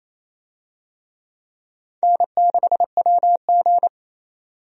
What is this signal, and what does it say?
Morse code, heard as a keyed tone, about 28 wpm, 700 Hz. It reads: N6WG
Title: USA Call Signs - 4 Characters - 28wpm